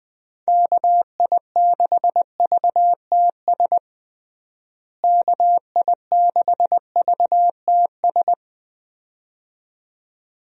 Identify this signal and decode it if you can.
Morse code, heard as a keyed tone, about 20 wpm, 700 Hz. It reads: KI6VTS KI6VTS
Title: USA Call Signs - 6 Characters - Repeated 2x - 20wpm